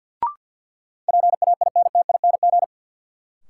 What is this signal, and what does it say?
Morse code, heard as a keyed tone, about 52 wpm, 700 Hz. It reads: PAINTING
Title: Sentences from Top 1200 Words - New Word Drill Alt - 50wpm